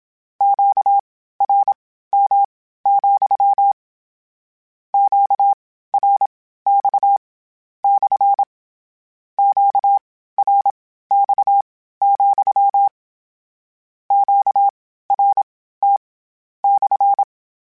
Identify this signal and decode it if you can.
Morse code, heard as a keyed tone, about 27 wpm, spaced as for 15 wpm, 800 Hz. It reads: QRM, QRX / QRX, QRT /
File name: Q Calls / Q Calls-mp3s / Q CALLS SPSL @25